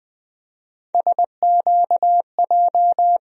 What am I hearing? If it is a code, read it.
Morse code, heard as a keyed tone, about 20 wpm, 700 Hz. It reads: SQJ
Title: USA Call Sign Postfixes - 3 Letters 20wpm